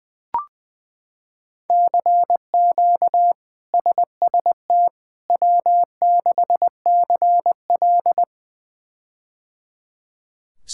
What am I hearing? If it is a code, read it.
Morse code, heard as a keyed tone, about 21 wpm, 700 Hz. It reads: CQ SST W6CL
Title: K1USN SST QSO Practice - 20wpm